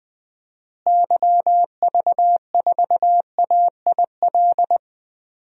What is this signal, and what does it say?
Morse code, heard as a keyed tone, about 20 wpm, 700 Hz. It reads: YV4AIL
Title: DXCC-100 Call Signs - 6 Characters - Repeated 2x - 20wpm